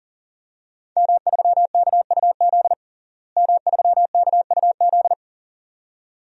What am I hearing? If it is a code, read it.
Morse code, heard as a keyed tone, about 40 wpm, 700 Hz. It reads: M3KUZ M3KUZ
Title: DXCC-100 Call Signs - 5 Characters - Repeated 2x - 40wpm